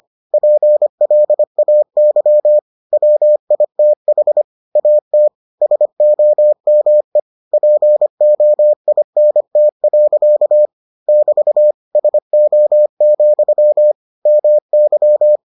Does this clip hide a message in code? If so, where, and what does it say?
Yes: Morse code from 0.3 to 15.4 s, heard as a keyed tone, about 25 wpm, 600 Hz. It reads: PLAY WITH AT SOME POINT. <BT> SO, MY